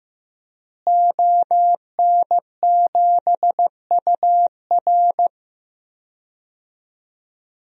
Morse code, heard as a keyed tone, about 16 wpm, 700 Hz. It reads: ON7UR
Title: DXCC-100 Call Signs - 5 Characters - Repeated 3x - 15wpm